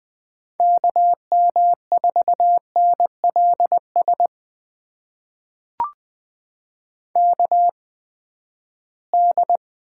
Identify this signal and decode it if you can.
Morse code, heard as a keyed tone, about 20 wpm, spaced as for 14 wpm, 700 Hz. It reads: KM4NLS KD
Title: USA Call Signs - 6 Characters - 8x Character Spacing - 20wpm